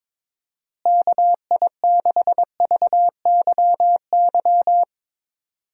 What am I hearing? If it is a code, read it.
Morse code, heard as a keyed tone, about 22 wpm, 700 Hz. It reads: KI6VYY
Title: USA Call Signs - 6 Characters - Repeated 3x - 22wpm